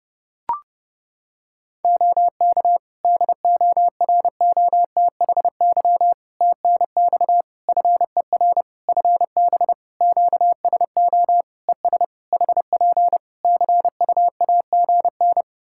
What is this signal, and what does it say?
Morse code, heard as a keyed tone, about 30 wpm, 700 Hz. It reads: OK DOROTHY TNX FER FB QSO ES HP CUAGN
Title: Practice QSO - Exchange 4 - Ending - 30wpm